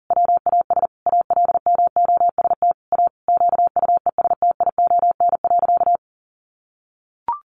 Morse code, heard as a keyed tone, about 40 wpm, 700 Hz. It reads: WAS ALMOST A QUESTION.
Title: Sentences from Top 1200 Words 40wpm